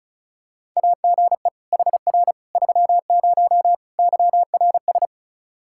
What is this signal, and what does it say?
Morse code, heard as a keyed tone, about 36 wpm, 700 Hz. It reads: AGE HR 30 YRS
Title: Practice QSO - Exchange 3 - More Info - 35wpm